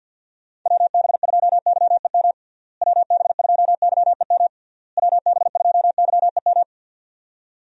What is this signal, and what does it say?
Morse code, heard as a keyed tone, about 48 wpm, 700 Hz. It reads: WB2YEM WB2YEM WB2YEM
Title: USA Call Signs - 6 Characters - Repeated 3x - 50wpm